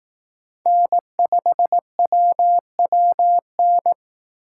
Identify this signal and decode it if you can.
Morse code, heard as a keyed tone, about 18 wpm, 700 Hz. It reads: N5WWN
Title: Speed Racing - USA Call Signs 15wpm